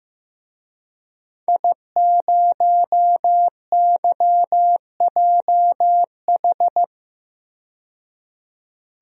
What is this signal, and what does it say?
Morse code, heard as a keyed tone, about 15 wpm, 700 Hz. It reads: I0YJH
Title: DXCC-100 Call Signs - 5 Characters - 15wpm